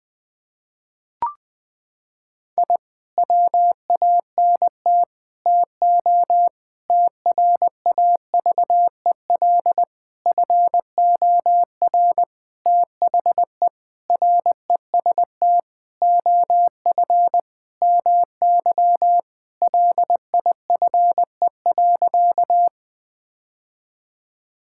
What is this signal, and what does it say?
Morse code, heard as a keyed tone, about 20 wpm, 700 Hz. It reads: I WANT TO TRAVEL FOR THE REST OF MY LIFE.
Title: Sentences with 5-Word Phrases - 20wpm